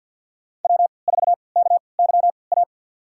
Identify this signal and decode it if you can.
Morse code, heard as a keyed tone, about 51 wpm, spaced as for 28 wpm, 690 Hz. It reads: W4KYA
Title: USA Call Signs - 5 Characters - 3x Character Spacing - 50wpm